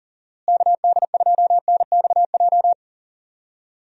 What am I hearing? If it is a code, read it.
Morse code, heard as a keyed tone, about 40 wpm, 700 Hz. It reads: KD2NXJ
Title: USA Call Signs - 6 Characters - Repeated 2x - 40wpm